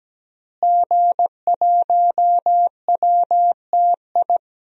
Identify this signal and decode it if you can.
Morse code, heard as a keyed tone, about 17 wpm, 700 Hz. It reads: G1WTI
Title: DXCC-100 Call Signs - 5 Characters - Repeated 2x - 17wpm